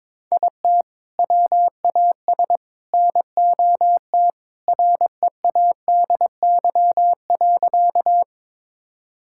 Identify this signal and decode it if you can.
Morse code, heard as a keyed tone, about 22 wpm, 700 Hz. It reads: IT WAS NOT READY.
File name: practice / 22wpm / Sentences with 2-Word Phrases - 22wpm